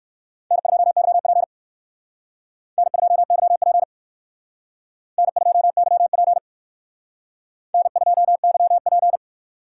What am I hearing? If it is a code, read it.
Morse code, heard as a keyed tone, about 63 wpm, 700 Hz. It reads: N2YP N2YP N2YP N2YP